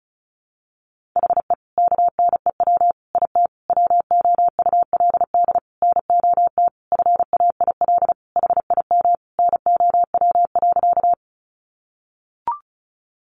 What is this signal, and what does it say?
Morse code, heard as a keyed tone, about 34 wpm, 700 Hz. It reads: HE KNEW IT WOULD NOT FAIL HIM NOW.
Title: Sentences from Top 1100 Words 35wpm